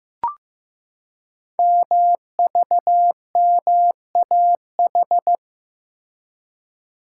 Morse code, heard as a keyed tone, about 15 wpm, 700 Hz. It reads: MVMAH
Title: Code Groups - 5 Letters - 15wpm